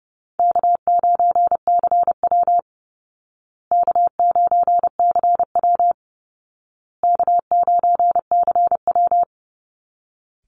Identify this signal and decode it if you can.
Morse code, heard as a keyed tone, about 30 wpm, 700 Hz. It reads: K9CW K9CW K9CW
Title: Call Signs Repeated 2x - 30wpm